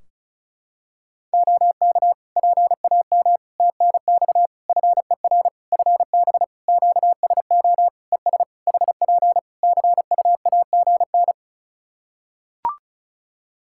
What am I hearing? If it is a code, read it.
Morse code, heard as a keyed tone, about 36 wpm, 700 Hz. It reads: OK PAM TNX FER FB QSO ES HP CUAGN
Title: K1USN SST QSO Practice 35wpm